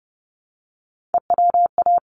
Morse code, heard as a keyed tone, about 30 wpm, 700 Hz. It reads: EWA
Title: ARRL Sections 30wpm